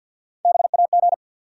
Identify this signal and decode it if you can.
Morse code, heard as a keyed tone, about 52 wpm, 700 Hz. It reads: BAG
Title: Three Letter Words - 50wpm